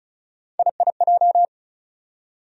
Morse code, heard as a keyed tone, about 34 wpm, 700 Hz. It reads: IIJ
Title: USA Call Sign Postfixes - 3 Letters 35wpm